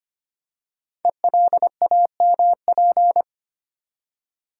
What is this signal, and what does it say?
Morse code, heard as a keyed tone, about 25 wpm, 700 Hz. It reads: ELAMP
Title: Code Groups - 5 Letters - 25wpm